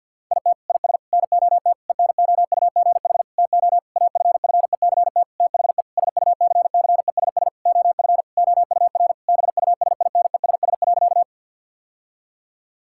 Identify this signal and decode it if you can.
Morse code, heard as a keyed tone, about 48 wpm, 700 Hz. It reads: IT IS NOT ENOUGH TO AFFECT THE SUCCESS OF OUR BUSINESS.
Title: Sentences from Top 1000 Words - Review Part 6 - 50wpm